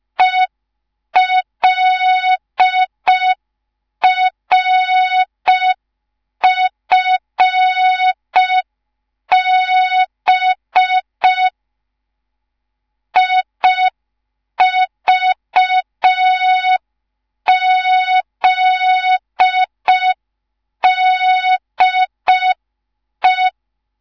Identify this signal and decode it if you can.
Morse code, heard as a keyed tone, about 5 wpm, 3700 Hz. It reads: ELRFB IVZDE
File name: Morze-HA2MN / hang / 10GWZ HA2MN